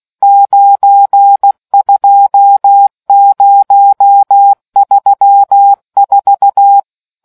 Morse code, heard as a keyed tone, about 16 wpm, 800 Hz. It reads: 92034